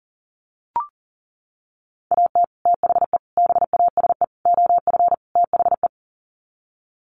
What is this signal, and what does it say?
Morse code, heard as a keyed tone, about 40 wpm, 700 Hz. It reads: AT THE BASE OF THE
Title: Sets of 5 Words 40wpm